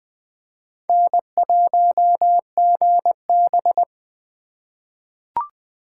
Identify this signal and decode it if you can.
Morse code, heard as a keyed tone, about 21 wpm, 700 Hz. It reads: N1GB
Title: 2023 SOTA Chaser Call Signs - 20wpm